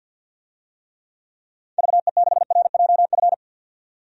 Morse code, heard as a keyed tone, about 48 wpm, 700 Hz. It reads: VE6RJF
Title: DXCC-100 Call Signs - 6 Characters - 50wpm